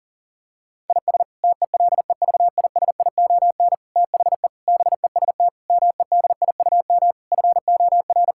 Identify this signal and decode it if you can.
Morse code, heard as a keyed tone, about 40 wpm, 700 Hz. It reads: IS TELEVISION THE BEST MEDIUM FOR
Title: Sentences from Top 1000 Words - Review Part 6 - 40wpm